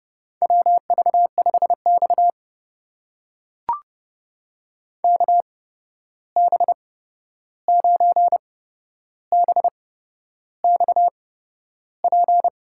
Morse code, heard as a keyed tone, about 30 wpm, spaced as for 11 wpm, 700 Hz. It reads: WV5X KB9BXP
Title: USA Call Signs - 8x Character Spacing - 30wpm